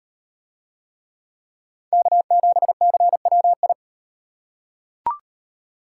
Morse code, heard as a keyed tone, about 38 wpm, 700 Hz. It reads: K7CWI